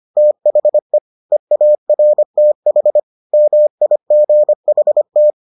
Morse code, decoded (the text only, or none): THE EARTH MIGHT